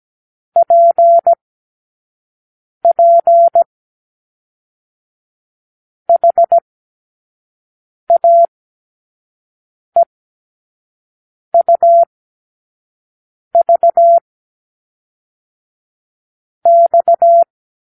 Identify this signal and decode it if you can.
Morse code, heard as a keyed tone, about 17 wpm, 680 Hz. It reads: P P H A E U V X
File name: k7qo / Lesson 36 (A-X)